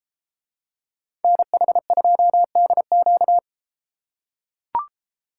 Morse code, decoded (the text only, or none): NH2DQ